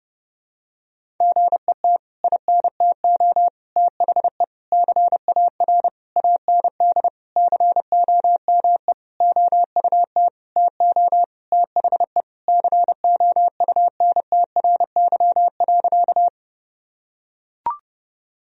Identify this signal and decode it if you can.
Morse code, heard as a keyed tone, about 30 wpm, 700 Hz. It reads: GET INTO THE CAR AND COME OUT TO THE COUNTRY.
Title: Sentences from Top 1000 Words - Review Part 1 - 30wpm